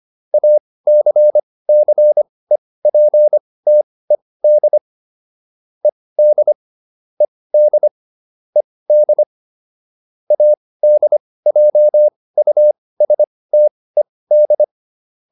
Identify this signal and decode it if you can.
Morse code, heard as a keyed tone, about 25 wpm, spaced as for 18 wpm, 600 Hz. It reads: ACCEPTED ED ED ED ADJUSTED